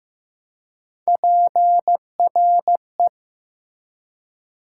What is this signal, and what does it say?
Morse code, heard as a keyed tone, about 15 wpm, 700 Hz. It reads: PRE
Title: Prefixes and Suffixes 15wpm